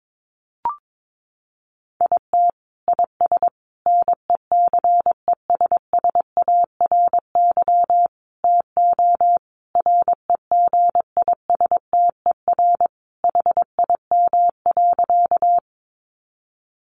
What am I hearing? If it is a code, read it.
Morse code, heard as a keyed tone, about 22 wpm, 700 Hz. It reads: IT IS NECESSARY TO REGISTER HIM.
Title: Sentences from Top 1200 Words 22wpm